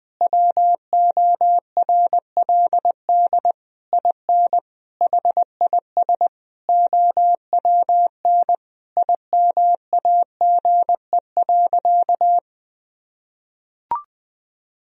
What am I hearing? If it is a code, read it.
Morse code, heard as a keyed tone, about 20 wpm, 700 Hz. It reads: WORLD IN HIS OWN IMAGE.
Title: Sentences from Top 1000 Words - Review Part 4 - 20wpm